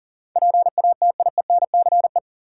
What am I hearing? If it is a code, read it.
Morse code, heard as a keyed tone, about 40 wpm, 700 Hz. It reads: PATIENCE
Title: Sentences from Top 1300 Words - New Word Drill - 40wpm